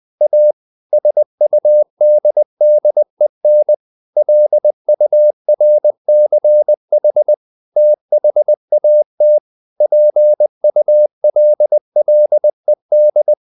Morse code, heard as a keyed tone, about 20 wpm, 600 Hz. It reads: A SUDDEN LURCH THAT PULLED
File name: the old man and the sea / the old man and the sea part 1 20wpm